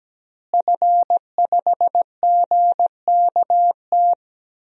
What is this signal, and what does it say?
Morse code, heard as a keyed tone, about 17 wpm, 700 Hz. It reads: F5GKT
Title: DXCC-100 Call Signs - 5 Characters - Repeated 3x - 17wpm